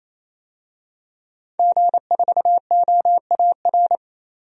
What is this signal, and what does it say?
Morse code, heard as a keyed tone, about 28 wpm, 700 Hz. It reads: G4OAR
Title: DXCC-100 Call Signs - 5 Characters - 28wpm